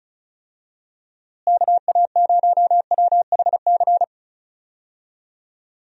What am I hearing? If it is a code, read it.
Morse code, heard as a keyed tone, about 34 wpm, 700 Hz. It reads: KA0WHC
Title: USA Call Signs - 6 Characters - Repeated 2x - 35wpm